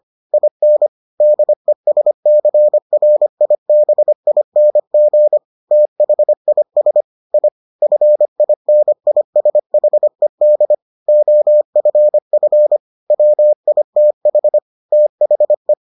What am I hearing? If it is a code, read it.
Morse code, heard as a keyed tone, about 25 wpm, 600 Hz. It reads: IN DESCRIBING THIS I FINISHED OFF WITH THE